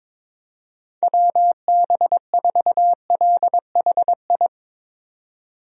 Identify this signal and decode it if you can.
Morse code, heard as a keyed tone, about 22 wpm, 700 Hz. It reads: WB4LHI
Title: USA Call Signs - 6 Characters - 22wpm